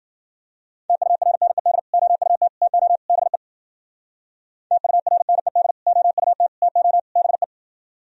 Morse code, heard as a keyed tone, about 64 wpm, 700 Hz. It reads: TURNED OUT TO BE TURNED OUT TO BE